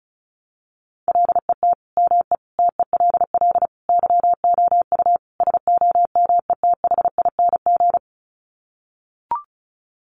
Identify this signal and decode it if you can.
Morse code, heard as a keyed tone, about 34 wpm, 700 Hz. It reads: LET ME TELL YOU SOMETHING
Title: Sets of 5 Words 35wpm